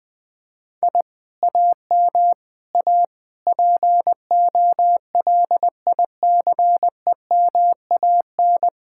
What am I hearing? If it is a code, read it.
Morse code, heard as a keyed tone, about 20 wpm, 700 Hz. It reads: I AM A POLICEMAN
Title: Speed-Racing Common QSO Phrases - 17wpm